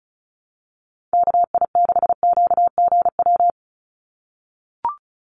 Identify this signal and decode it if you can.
Morse code, heard as a keyed tone, about 36 wpm, 700 Hz. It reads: KI6QGW
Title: USA Call Signs 35wpm